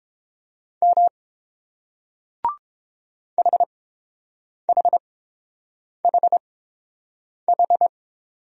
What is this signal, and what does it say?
Morse code, heard as a keyed tone, about 30 wpm, spaced as for 7 wpm, 700 Hz. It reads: M HHHH